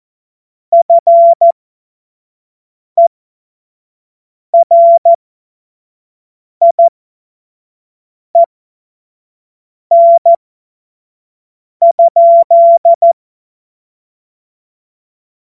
Morse code, heard as a keyed tone, about 14 wpm, spaced as for 5 wpm, 680 Hz. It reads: FERIEN?
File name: Morsekurs / Witze / Witz 14. März 2023 25 F70